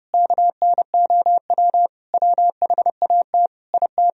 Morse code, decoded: KNOW WHAT IT

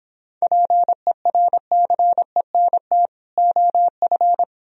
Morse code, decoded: PERCENT OF

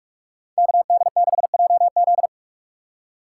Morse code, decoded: KD6JZ

